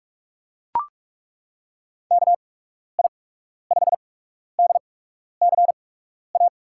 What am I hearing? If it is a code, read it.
Morse code, heard as a keyed tone, about 42 wpm, spaced as for 12 wpm, 700 Hz. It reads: KI5DCA
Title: USA Call Signs - 8x Character Spacing - 45wpm